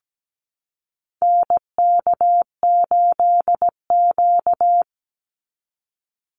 Morse code, heard as a keyed tone, about 17 wpm, 700 Hz. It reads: NK8Q
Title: NA SOTA Activator Callsigns 17wpm